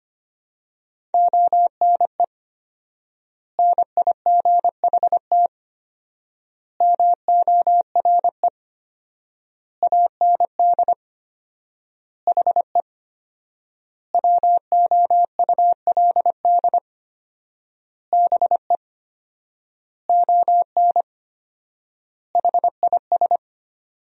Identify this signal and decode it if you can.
Morse code, heard as a keyed tone, about 25 wpm, 700 Hz. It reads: ONE NIGHT MORE AND HE WOULD BE ON HIS
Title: 4x Word-Spacing - Sentences from Top 300 Words - 25wpm